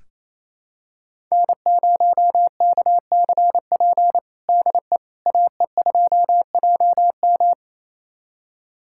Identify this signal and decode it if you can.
Morse code, heard as a keyed tone, about 28 wpm, 700 Hz. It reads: N0KCP DE AE2JM
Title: USA Callsign DE Callsign 28wpm